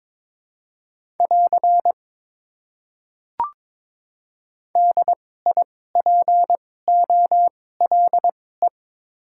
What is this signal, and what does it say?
Morse code, heard as a keyed tone, about 22 wpm, spaced as for 16 wpm, 700 Hz. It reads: <AR> DIPOLE
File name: Top 90 QSO Elements - 2x Character Spacing - 22wpm